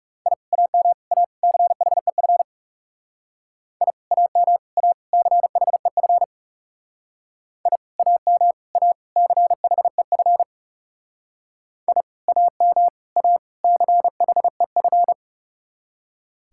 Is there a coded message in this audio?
Yes: Morse code from 0.3 to 15.1 s, heard as a keyed tone, about 38 wpm, 700 Hz. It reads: I AM A CHEF I AM A CHEF I AM A CHEF I AM A CHEF